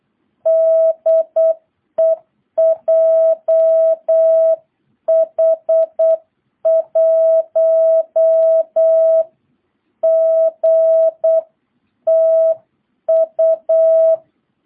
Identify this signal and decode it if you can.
Morse code, heard as a keyed tone, about 8 wpm, 660 Hz. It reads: DE JH1GTU